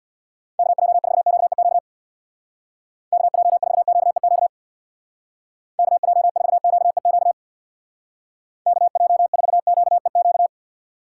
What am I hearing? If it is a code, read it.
Morse code, heard as a keyed tone, about 61 wpm, 700 Hz. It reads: KJ4QEQ KJ4QEQ KJ4QEQ KJ4QEQ
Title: Speed Racing - USA 2x3 Call Signs 50wpm